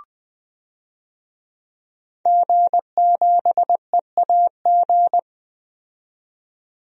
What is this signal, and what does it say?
Morse code, heard as a keyed tone, about 20 wpm, 700 Hz. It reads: G7EAG
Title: DXCC-100 Call Signs - 5 Characters - 20wpm